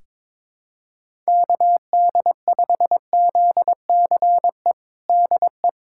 Morse code, heard as a keyed tone, about 22 wpm, 700 Hz. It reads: KD5ZCE DE
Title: USA Callsign DE Callsign - 6 Characters - 22wpm